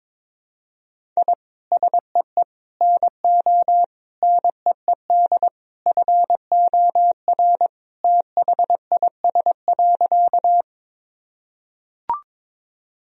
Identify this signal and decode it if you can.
Morse code, heard as a keyed tone, about 22 wpm, 700 Hz. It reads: I SEE NO NEED FOR THIS.
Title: Sentences from Top 1000 Words - Review Part 1 - 22wpm